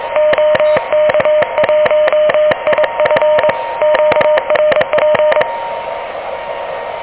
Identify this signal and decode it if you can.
Morse code, heard as a keyed tone, about 22 wpm, 600 Hz, with noise. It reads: OK1IF QRP
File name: OK QRP 2023 / OK QRP 2023